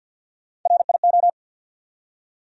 Morse code, heard as a keyed tone, about 54 wpm, 700 Hz. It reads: RIO
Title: USA Call Sign Postfixes - 3 Letters 50wpm